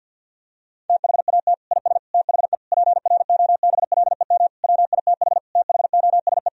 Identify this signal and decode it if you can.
Morse code, heard as a keyed tone, about 52 wpm, 700 Hz. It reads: THAT IS THE PROBLEM WITH THOSE
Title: Sentences from Top 1000 Words - Review Part 2 - 50wpm